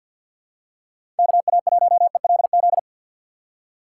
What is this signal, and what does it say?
Morse code, heard as a keyed tone, about 48 wpm, 700 Hz. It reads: KA1ELZ
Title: USA Call Signs - 6x Character Spacing - 50wpm